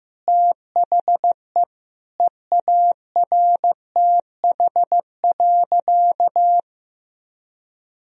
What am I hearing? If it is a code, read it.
Morse code, heard as a keyed tone, about 16 wpm, 700 Hz. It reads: THE EARTH.
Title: Sentences from Top 1000 Words - Review Part 1 - 15wpm